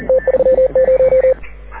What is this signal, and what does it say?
Morse code, heard as a keyed tone, about 41 wpm, 520 Hz. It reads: T30
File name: AUDIO2019 / HG6N CW SSB MIX